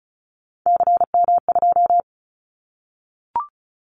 Morse code, heard as a keyed tone, about 36 wpm, 700 Hz. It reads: CM2
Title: Callsign Prefixes for Top 100 DXCC - 35wpm